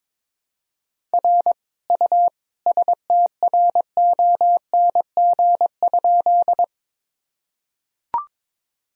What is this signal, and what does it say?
Morse code, heard as a keyed tone, about 22 wpm, 700 Hz. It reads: R U STRONG?
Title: Sentences from Top 500 Words and Abbreviations 22wpm